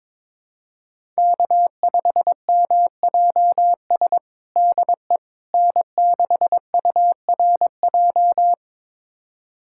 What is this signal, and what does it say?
Morse code, heard as a keyed tone, about 22 wpm, 700 Hz. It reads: K5MJS DE N6URJ
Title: USA Callsign DE Callsign - 5 Characters - 22wpm